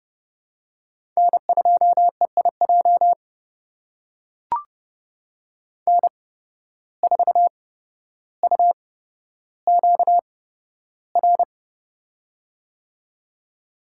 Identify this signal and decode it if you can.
Morse code, heard as a keyed tone, about 30 wpm, spaced as for 13 wpm, 700 Hz. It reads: N2EIJ N4UQR
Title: USA Call Signs - 5 Characters - 8x Character Spacing - 30wpm